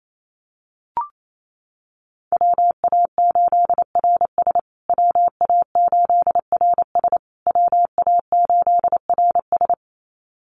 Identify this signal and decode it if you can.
Morse code, heard as a keyed tone, about 28 wpm, 700 Hz. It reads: WA8RS WA8RS WA8RS
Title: USA Call Signs Repeated 3x 28wpm